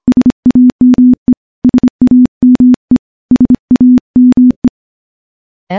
Morse code, heard as a keyed tone, about 26 wpm, 250 Hz. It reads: SAME SAME SAME
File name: morse-training-25WPM